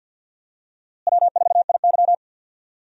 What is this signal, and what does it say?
Morse code, heard as a keyed tone, about 47 wpm, 700 Hz. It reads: W4IY
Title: USA Call Signs - 4 Characters - Repeated 3x - 50wpm